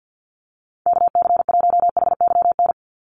Morse code, heard as a keyed tone, about 50 wpm, 700 Hz. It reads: KC1HYN
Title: USA Call Signs 50wpm